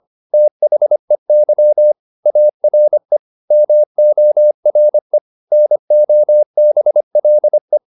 Morse code, heard as a keyed tone, about 25 wpm, 600 Hz. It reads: THEY ARE MORE NOBLE
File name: the old man and the sea part 1 25wpm